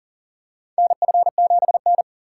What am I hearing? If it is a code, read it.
Morse code, heard as a keyed tone, about 40 wpm, 700 Hz. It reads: NF7N
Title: USA Call Signs - 4 Characters - 2x Character Spacing - 40wpm